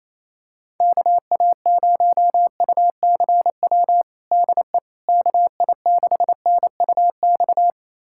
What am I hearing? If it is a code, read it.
Morse code, heard as a keyed tone, about 27 wpm, 700 Hz. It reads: KA0UCW DE KI6NUX
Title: USA Callsign DE Callsign - 6 Characters - 28wpm